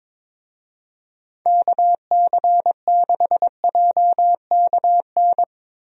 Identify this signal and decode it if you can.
Morse code, heard as a keyed tone, about 22 wpm, 700 Hz. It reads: KC6JKN
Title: USA Call Signs - 6 Characters - Repeated 2x - 22wpm